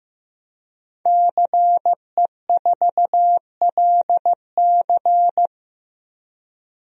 Morse code, heard as a keyed tone, about 15 wpm, 700 Hz. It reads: CE4LC
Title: DXCC-100 Call Signs - 5 Characters - Repeated 2x - 15wpm